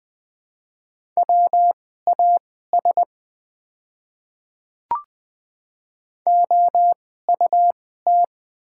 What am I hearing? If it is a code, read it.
Morse code, heard as a keyed tone, about 20 wpm, spaced as for 15 wpm, 690 Hz. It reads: WAS OUT
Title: 2x Character Spacing - Top 100 Words - 20wpm